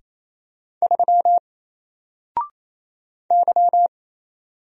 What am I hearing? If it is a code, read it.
Morse code, heard as a keyed tone, about 27 wpm, 700 Hz. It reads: 3 Y
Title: Single Letter-Number Alt 28wpm